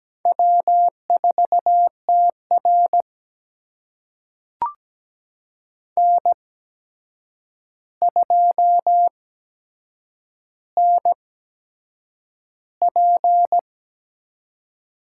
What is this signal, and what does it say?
Morse code, heard as a keyed tone, about 17 wpm, spaced as for 7 wpm, 700 Hz. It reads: W4TR N2NP
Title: USA Call Signs - 4 Characters - 8x Character Spacing - 17wpm